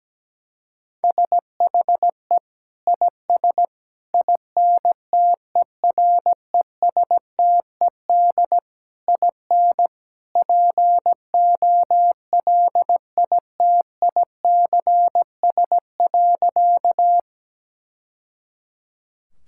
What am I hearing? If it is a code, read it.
Morse code, heard as a keyed tone, about 17 wpm, 700 Hz. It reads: SHE IS INTERESTED IN POLITICS.
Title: Sentences from Top 1000 Words - Review Part 6 - 17wpm